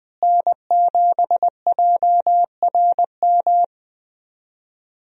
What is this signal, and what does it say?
Morse code, heard as a keyed tone, about 21 wpm, 700 Hz. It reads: N7JRM
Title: USA Call Signs - 5 Characters - Repeated 2x - 20wpm